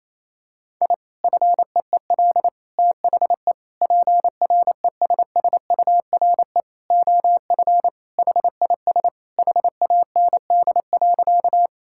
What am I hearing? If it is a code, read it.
Morse code, heard as a keyed tone, about 28 wpm, 700 Hz. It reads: I FEEL THE PRESSURE OF HIS HAND.
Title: Sentences from Top 1000 Words - Review Part 6 - 28wpm